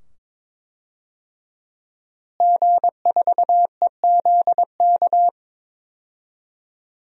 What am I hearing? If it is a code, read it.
Morse code, heard as a keyed tone, about 22 wpm, 700 Hz. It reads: G4EZK
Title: DXCC-100 Call Signs - 5 Characters - Repeated 2x - 22wpm